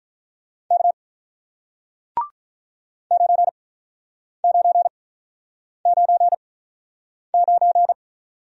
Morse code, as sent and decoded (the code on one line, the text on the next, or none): -.- / ----. ----. ----. ----.
K 9999